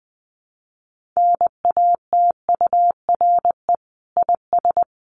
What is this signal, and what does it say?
Morse code, heard as a keyed tone, about 20 wpm, 700 Hz. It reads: NATURE IS
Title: Sentences from Top 1500 Words 20wpm